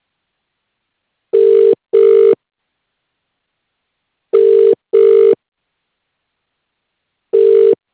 Ringback tone (UK), 400 + 450 Hz.